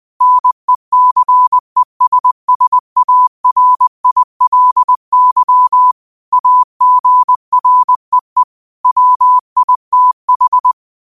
Morse code, heard as a keyed tone, about 20 wpm, 1000 Hz. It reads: NECESSARILY AGREE WITH
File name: qotdfiles / 20wpm